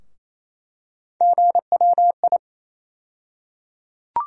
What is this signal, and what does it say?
Morse code, heard as a keyed tone, about 28 wpm, 700 Hz. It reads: GWI
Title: USA Call Sign Postfixes - 3 Letters 28wpm